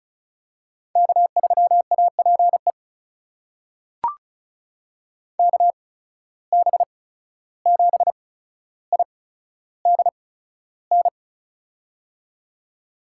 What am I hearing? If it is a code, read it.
Morse code, heard as a keyed tone, about 36 wpm, spaced as for 14 wpm, 700 Hz. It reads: K3APE KB7IDN
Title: USA Call Signs - 8x Character Spacing - 35wpm